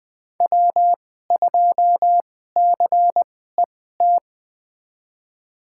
Morse code, heard as a keyed tone, about 20 wpm, spaced as for 14 wpm, 700 Hz. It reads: W2CET
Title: USA Call Signs - 5 Characters - 2x Character Spacing - 20wpm